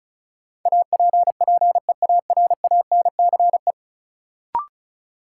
Morse code, heard as a keyed tone, about 34 wpm, 700 Hz. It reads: APPEARANCE
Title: Sentences from Top 1100 Words - New Word Drill - 35wpm